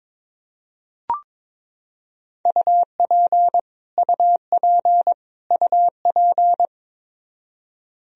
Morse code, heard as a keyed tone, about 22 wpm, 700 Hz. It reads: UP UP UP
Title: Top 100 Words Repeated 22wpm